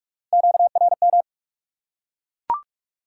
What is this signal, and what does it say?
Morse code, heard as a keyed tone, about 44 wpm, 700 Hz. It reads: QRM
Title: Top 180 QSO Elements 45wpm